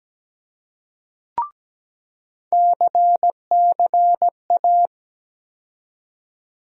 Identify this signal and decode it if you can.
Morse code, heard as a keyed tone, about 17 wpm, 700 Hz. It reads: CCA
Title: USA Call Sign Postfixes - 3 Letters 17wpm